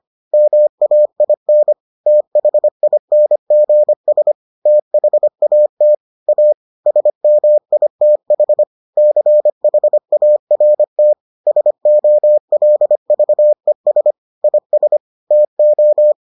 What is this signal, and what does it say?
Morse code, heard as a keyed tone, about 25 wpm, 600 Hz. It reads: MAIN THINGS THAT A SMITH CHART SOLVES IS TO